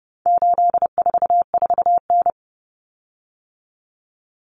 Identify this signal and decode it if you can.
Morse code, heard as a keyed tone, about 30 wpm, 700 Hz. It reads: 844N
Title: Common Cut Numbers - 4-Digit 30wpm